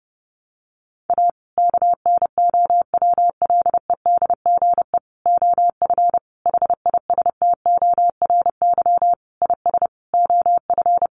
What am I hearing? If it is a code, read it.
Morse code, heard as a keyed tone, about 28 wpm, 700 Hz. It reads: A KNOWLEDGE OF HISTORY IS OF
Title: Sentences from Top 900 Words 30wpm